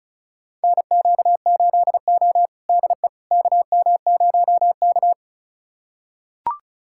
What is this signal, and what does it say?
Morse code, heard as a keyed tone, about 36 wpm, 700 Hz. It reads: NQ8O DE KM0K
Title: USA Callsign DE Callsign - 4 Characters - 35wpm